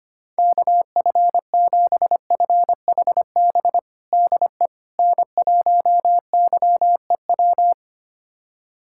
Morse code, heard as a keyed tone, about 25 wpm, 700 Hz. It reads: KF7FHB DE N1YEW
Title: USA Callsign DE Callsign 25wpm